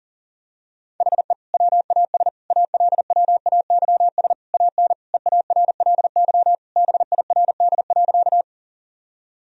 Morse code, heard as a keyed tone, about 37 wpm, 700 Hz. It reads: HE WAS ALWAYS AN EARLY BIRD.
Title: Sentences from Top 1000 Words - Review Part 2 - 40wpm